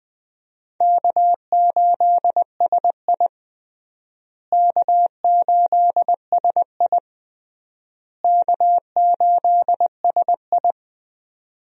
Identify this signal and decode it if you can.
Morse code, heard as a keyed tone, about 20 wpm, 700 Hz. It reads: K8SI K8SI K8SI
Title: USA Call Signs - 4 Characters - Repeated 3x - 20wpm